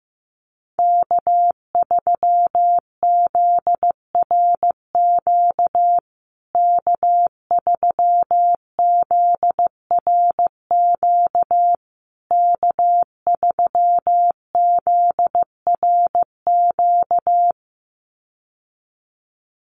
Morse code, heard as a keyed tone, about 14 wpm, 700 Hz. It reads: K3ZRQ K3ZRQ K3ZRQ
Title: USA Call Signs Repeated 3x 15wpm